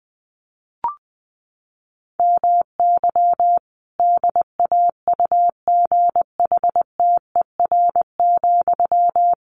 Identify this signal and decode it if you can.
Morse code, heard as a keyed tone, about 20 wpm, 700 Hz. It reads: MY DAUGHTER,
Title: Sentences from Top 1500 Words 20wpm